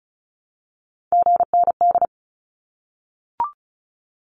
Morse code, heard as a keyed tone, about 34 wpm, 700 Hz. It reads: GND
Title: Abbreviations Repeated 3x - 35wpm